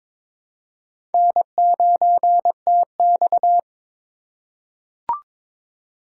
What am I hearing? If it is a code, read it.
Morse code, heard as a keyed tone, about 22 wpm, 700 Hz. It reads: N9TX
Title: USA Call Signs - 4 Characters - 6x Character Spacing - 22wpm